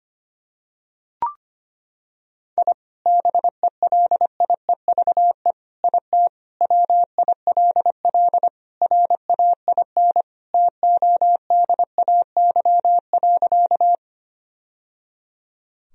Morse code, heard as a keyed tone, about 25 wpm, 700 Hz. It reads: I BELIEVE IT WILL RAIN TODAY.
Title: Sentences from Top 1000 Words - Review Part 1 - 25wpm